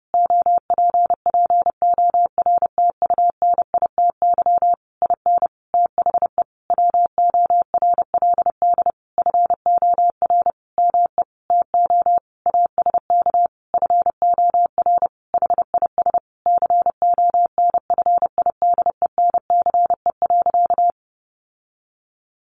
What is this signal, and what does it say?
Morse code, heard as a keyed tone, about 30 wpm, 700 Hz. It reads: OPPORTUNITY IN THE WORLD FOR ME TO ASK FOR HIS CONFIDENCE.
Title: Sentences from Top 1500 Words 30wpm